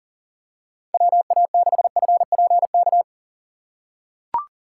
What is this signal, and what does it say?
Morse code, heard as a keyed tone, about 40 wpm, 700 Hz. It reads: WA6FPK
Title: USA Call Signs - 2x Character Spacing - 40wpm